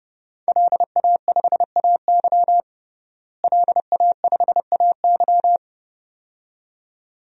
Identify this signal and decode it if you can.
Morse code, heard as a keyed tone, about 30 wpm, 700 Hz. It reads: LA5AY LA5AY
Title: DXCC-100 Call Signs - 5 Characters - Repeated 2x - 30wpm